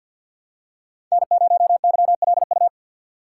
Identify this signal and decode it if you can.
Morse code, heard as a keyed tone, about 48 wpm, 700 Hz. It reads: N0YLU
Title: USA Call Signs - 5 Characters - 3x Character Spacing - 50wpm